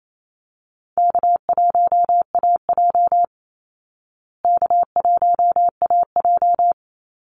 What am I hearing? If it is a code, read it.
Morse code, heard as a keyed tone, about 28 wpm, 700 Hz. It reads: K1AJ K1AJ
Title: Call Signs Repeated 3x - 28wpm